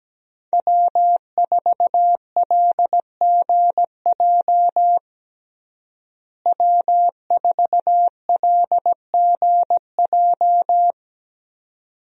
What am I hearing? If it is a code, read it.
Morse code, heard as a keyed tone, about 17 wpm, 700 Hz. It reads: W4LGJ W4LGJ
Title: USA Call Signs - 5 Characters - Repeated 2x - 17wpm